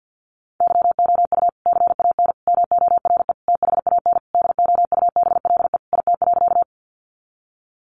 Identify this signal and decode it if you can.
Morse code, heard as a keyed tone, about 50 wpm, 700 Hz. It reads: YOU CAN MORE THAN DOUBLE IT.
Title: Sentences from Top 1400 Words 50wpm